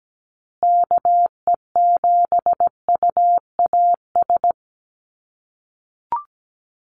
Morse code, heard as a keyed tone, about 17 wpm, 700 Hz. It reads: KE7UAS
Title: USA Call Signs Repeated 3x 17wpm